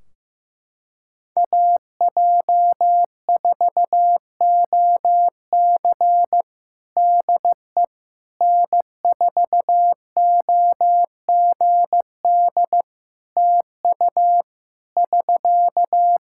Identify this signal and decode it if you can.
Morse code, heard as a keyed tone, about 15 wpm, 700 Hz. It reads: AJ4OC DE N4OGD TU <SK>
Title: Practice QSO - Exchange 4 - Ending - 15wpm